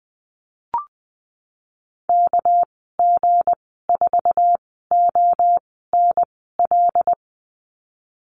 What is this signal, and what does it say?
Morse code, heard as a keyed tone, about 20 wpm, spaced as for 14 wpm, 700 Hz. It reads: KG4ONL
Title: 200 Lesson 32g - Letter k - Call Signs 20wpm